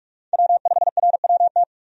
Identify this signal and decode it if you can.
Morse code, heard as a keyed tone, about 46 wpm, 700 Hz. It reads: W5RWT
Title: USA Call Signs - 5 Characters - 45wpm